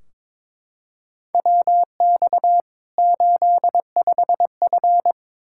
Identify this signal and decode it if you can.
Morse code, heard as a keyed tone, about 22 wpm, 700 Hz. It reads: WX 85F
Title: Practice QSO - Exchange 2 - Info - 22wpm